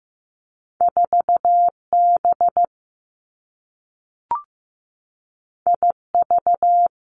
Morse code, heard as a keyed tone, about 15 wpm, 700 Hz. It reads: 4 B I V
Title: Two Challenging Letter-Number Combinations 15wpm